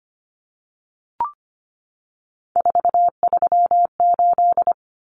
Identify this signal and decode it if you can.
Morse code, heard as a keyed tone, about 25 wpm, 700 Hz. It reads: 438
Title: Common Cut Numbers - 3-Digit 25wpm